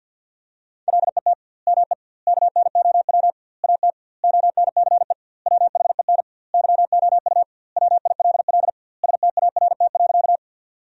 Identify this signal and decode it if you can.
Morse code, heard as a keyed tone, about 50 wpm, 700 Hz. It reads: LET ME KNOW AT ONCE WHEN YOU WILL START.
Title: 2x Word-Spacing - Sentences from Top 300 Words - 50wpm